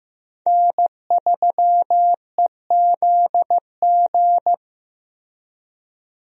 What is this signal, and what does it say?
Morse code, heard as a keyed tone, about 16 wpm, 700 Hz. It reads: N3EZG
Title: USA Call Signs - 5 Characters - Repeated 3x - 15wpm